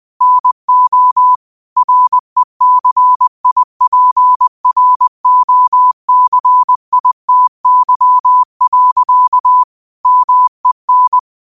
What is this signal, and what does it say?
Morse code, heard as a keyed tone, about 20 wpm, 1000 Hz. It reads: NO RECIPROCITY. MEN
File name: qotdfiles / 20wpm